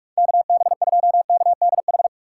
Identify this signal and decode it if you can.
Morse code, heard as a keyed tone, about 45 wpm, 700 Hz. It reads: KB2KDH